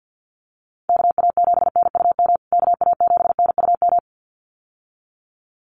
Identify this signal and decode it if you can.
Morse code, heard as a keyed tone, about 50 wpm, 700 Hz. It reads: KA7NUM KA7NUM
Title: USA Call Signs Repeated 2x 50wpm